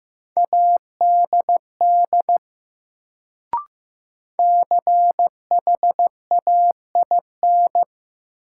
Morse code, heard as a keyed tone, about 15 wpm, 700 Hz. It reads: ADD CHAIN
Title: Sentences from Top 1500 Words - New Word Drill Alt - 15wpm